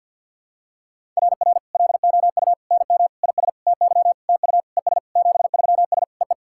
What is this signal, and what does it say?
Morse code, heard as a keyed tone, about 49 wpm, 700 Hz. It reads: RR LOU NM IS TY TU ES 73S EE
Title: SOTA QSO Practice 50wpm